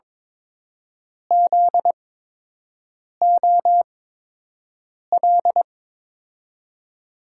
Morse code, heard as a keyed tone, about 22 wpm, spaced as for 6 wpm, 700 Hz. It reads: ZOL
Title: USA Call Signs - 5 Characters - 8x Character Spacing - 22wpm